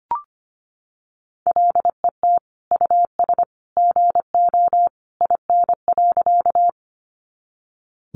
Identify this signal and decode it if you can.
Morse code, heard as a keyed tone, about 25 wpm, 700 Hz. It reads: LET US GO IN.
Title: Sentences from Top 300 Words 25wpm